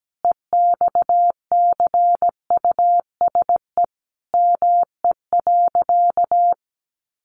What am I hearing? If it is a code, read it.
Morse code, heard as a keyed tone, about 17 wpm, 700 Hz. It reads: EXCUSE ME.
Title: Sentences from Top 700 Words 17wpm